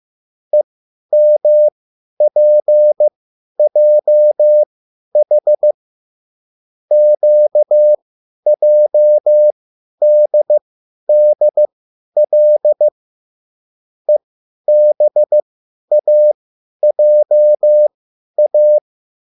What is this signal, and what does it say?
Morse code, heard as a keyed tone, about 15 wpm, spaced as for 10 wpm, 600 Hz. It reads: EMPJH QJDDL EBAJA